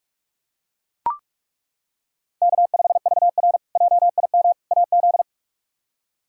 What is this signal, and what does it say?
Morse code, heard as a keyed tone, about 41 wpm, 700 Hz. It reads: K5VR JIM AZ
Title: K1USN SST QSO Practice - 45wpm